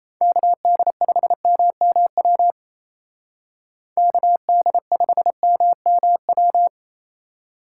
Morse code, heard as a keyed tone, about 36 wpm, 700 Hz. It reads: KD5MMW KD5MMW